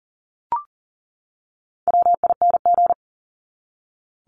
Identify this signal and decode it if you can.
Morse code, heard as a keyed tone, about 40 wpm, 700 Hz. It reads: WING
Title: Top 1001-1250 Words 40wpm